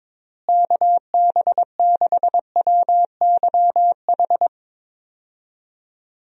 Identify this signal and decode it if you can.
Morse code, heard as a keyed tone, about 21 wpm, 700 Hz. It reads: KB6WYH
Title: USA Call Signs - 6 Characters - Repeated 3x - 22wpm